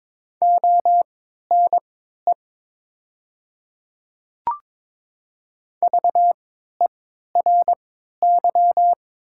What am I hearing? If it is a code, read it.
Morse code, heard as a keyed tone, about 22 wpm, spaced as for 12 wpm, 700 Hz. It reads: ONE VERY
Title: 3x Character Spacing - Top 200 Words - 22wpm